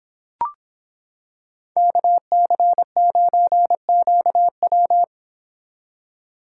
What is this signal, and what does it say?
Morse code, heard as a keyed tone, about 26 wpm, 700 Hz. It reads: KC9QW